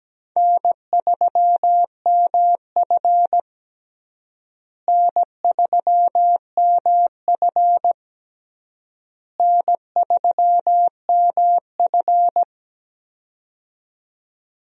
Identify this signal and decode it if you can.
Morse code, heard as a keyed tone, about 17 wpm, 700 Hz. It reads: N3MF N3MF N3MF
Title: USA Call Signs - 4 Characters - Repeated 3x - 17wpm